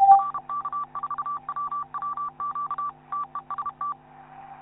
Morse code, heard as a keyed tone, about 32 wpm, 1160 Hz, with noise. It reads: RK4WWQ TEST
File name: QSO clips / YI9PSE-20m- -0519Z